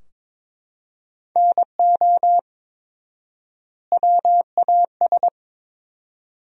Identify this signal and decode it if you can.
Morse code, heard as a keyed tone, about 21 wpm, 700 Hz. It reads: NO WAS
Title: 4x Word-Spacing - Sentences from Top 100 Words - 22wpm